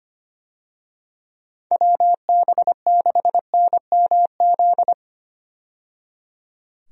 Morse code, heard as a keyed tone, about 25 wpm, 700 Hz. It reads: WB6NMZ
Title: USA Call Signs - 6 Characters - 25wpm